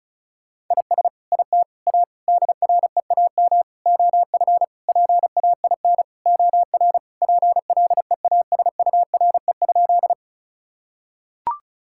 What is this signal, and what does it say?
Morse code, heard as a keyed tone, about 36 wpm, 700 Hz. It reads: IS IT A DREAM OF PAIN OR PLEASURE?
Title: Sentences from Top 1600 Words 35wpm